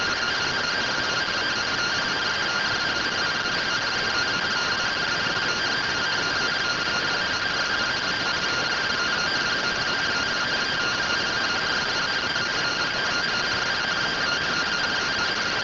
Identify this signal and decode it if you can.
Two-tone teleprinter signal, about 170 Hz shift.